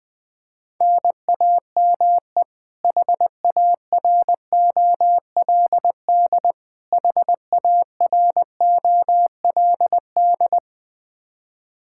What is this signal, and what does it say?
Morse code, heard as a keyed tone, about 20 wpm, 700 Hz. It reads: NAME HAROLD HAROLD
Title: K1USN SST QSO Practice 20wpm